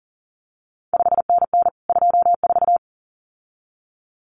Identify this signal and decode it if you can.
Morse code, heard as a keyed tone, about 39 wpm, 700 Hz. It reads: 5NN 24
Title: CQ WW DX Contest Exchange Encore - 40wpm